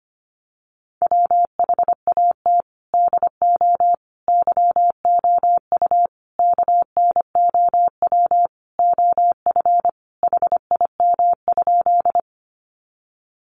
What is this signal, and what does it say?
Morse code, heard as a keyed tone, about 25 wpm, 700 Hz. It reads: WHAT DO YOU KNOW OF HIM?
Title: Sentences from Top 100 Words 25wpm